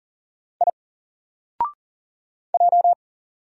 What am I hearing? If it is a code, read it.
Morse code, heard as a keyed tone, about 40 wpm, 700 Hz. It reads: I J
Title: Single Character (Letter:Number:Punctuation:Prosign) 40wpm